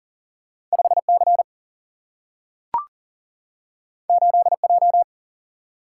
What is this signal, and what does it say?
Morse code, heard as a keyed tone, about 40 wpm, 700 Hz. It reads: 5 C 8 J
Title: Single-Letter Single-Number 40wpm